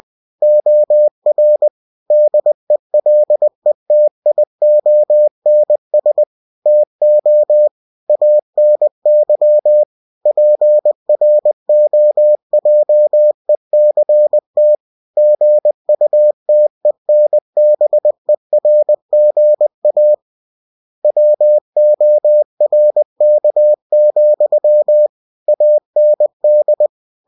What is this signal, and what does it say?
Morse code, heard as a keyed tone, about 20 wpm, 600 Hz. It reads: OR DELETIONS TO ANY PROJECT GUTENBERGA WORK, AND
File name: cw-books / winnie-the-pooh / winnie-20wpm-page-0120